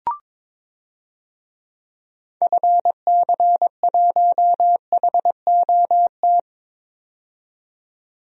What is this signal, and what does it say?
Morse code, heard as a keyed tone, about 22 wpm, 700 Hz. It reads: FC1HOT
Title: DXCC-100 Call Signs - 6 Characters - 22wpm